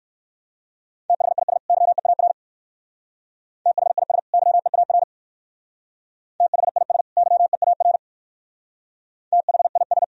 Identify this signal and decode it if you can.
Morse code, heard as a keyed tone, about 56 wpm, 700 Hz. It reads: THIS YEAR THIS YEAR THIS YEAR THIS